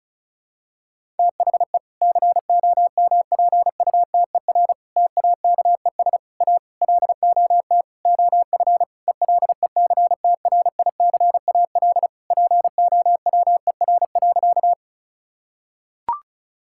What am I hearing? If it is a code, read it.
Morse code, heard as a keyed tone, about 36 wpm, 700 Hz. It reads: THE COMPUTER TAKES A LOT OF ELECTRICAL POWER.
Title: Sentences from Top 1000 Words - Review Part 6 - 35wpm